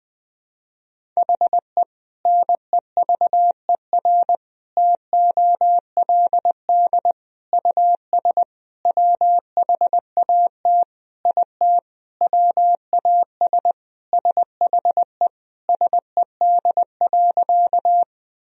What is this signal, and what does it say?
Morse code, heard as a keyed tone, about 21 wpm, 700 Hz. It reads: HE NEVER TOLD US WHAT IT WAS SHE SED.
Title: Sentences from Top 400 Words and Abbreviations 20wpm